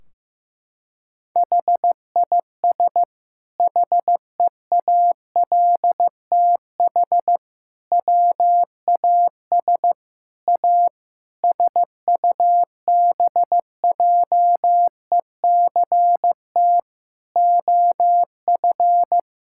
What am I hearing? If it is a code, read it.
Morse code, heard as a keyed tone, about 16 wpm, 700 Hz. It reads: HIS HEALTH WAS A SUBJECT OF
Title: Sentences from Top 1600 Words 15wpm